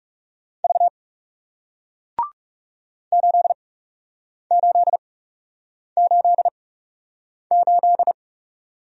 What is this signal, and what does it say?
Morse code, heard as a keyed tone, about 37 wpm, spaced as for 8 wpm, 700 Hz. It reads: V 8888
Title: Speed-Racing Single Letter-Number 30wpm